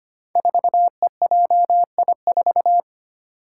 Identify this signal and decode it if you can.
Morse code, heard as a keyed tone, about 25 wpm, 700 Hz. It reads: 4EJI4
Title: Five Challenging Letter-Number Combinations 25wpm